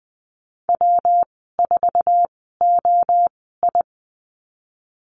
Morse code, heard as a keyed tone, about 20 wpm, spaced as for 15 wpm, 700 Hz. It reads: W4OI
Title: 186 Lesson 30g - Letter b - Call Signs 20wpm